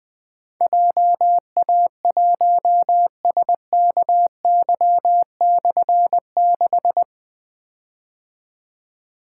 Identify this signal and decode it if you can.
Morse code, heard as a keyed tone, about 20 wpm, 700 Hz. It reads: JA1SKY/6
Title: Difficult Call Signs 20wpm